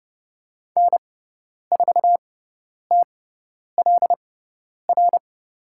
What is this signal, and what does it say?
Morse code, heard as a keyed tone, about 30 wpm, spaced as for 10 wpm, 700 Hz. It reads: N4TLR
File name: Lesson 18g - Letter m - Call Signs 30wpm